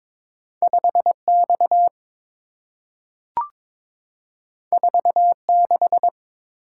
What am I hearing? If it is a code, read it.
Morse code, heard as a keyed tone, about 22 wpm, 690 Hz. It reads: 5 X 4 6
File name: practice / 22wpm / Two Letter-Number Combinations - 22wpm